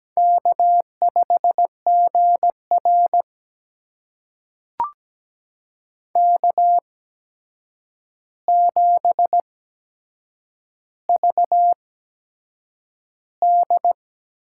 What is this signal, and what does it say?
Morse code, heard as a keyed tone, about 17 wpm, spaced as for 7 wpm, 700 Hz. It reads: K5GR K7VD
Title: USA Call Signs - 4 Characters - 8x Character Spacing - 17wpm